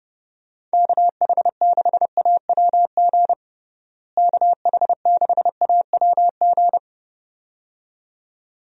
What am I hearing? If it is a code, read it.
Morse code, heard as a keyed tone, about 32 wpm, 700 Hz. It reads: KH6AWG KH6AWG
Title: DXCC-100 Call Signs - 6 Characters - Repeated 2x - 30wpm